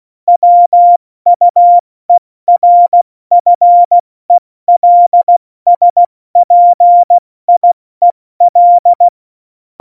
Morse code, heard as a keyed tone, about 16 wpm, spaced as for 14 wpm, 700 Hz. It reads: WUERFELSPIEL
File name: Morsekurs / Witze / Witz 22. Dezember 2020, 70 F80